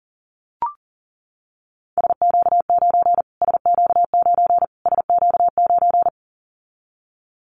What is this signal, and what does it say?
Morse code, heard as a keyed tone, about 40 wpm, 700 Hz. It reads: SQ9 SQ9 SQ9
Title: Callsign Prefixes for Top 25 DXCC - Repeated 3x - 40wpm